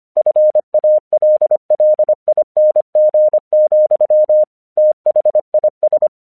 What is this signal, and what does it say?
Morse code, heard as a keyed tone, about 25 wpm, 610 Hz. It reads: FALLING, THIS
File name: 1 English 25wpm / AQC02